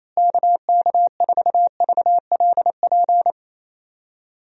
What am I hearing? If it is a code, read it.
Morse code, heard as a keyed tone, about 28 wpm, 700 Hz. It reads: KK4VLP
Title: USA Call Signs - 6 Characters - Repeated 3x - 28wpm